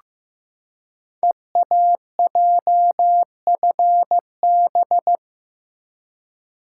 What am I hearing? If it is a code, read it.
Morse code, heard as a keyed tone, about 15 wpm, 700 Hz. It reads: EAJFB
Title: Code Groups - 5 Letter:Numbers - 15wpm